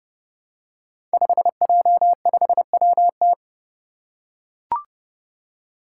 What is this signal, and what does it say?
Morse code, heard as a keyed tone, about 30 wpm, 700 Hz. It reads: 5J5WT
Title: Five Challenging Letter-Number Combinations 30wpm